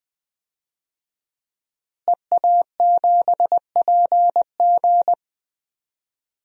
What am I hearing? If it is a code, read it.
Morse code, heard as a keyed tone, about 20 wpm, 700 Hz. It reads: EA7PG
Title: DXCC-100 Call Signs - 5 Characters - Repeated 3x - 20wpm